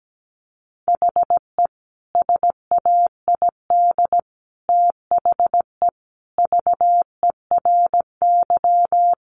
Morse code, heard as a keyed tone, about 17 wpm, 700 Hz. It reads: HE SAID THE VERY
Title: Sentences from Top 1300 Words 17wpm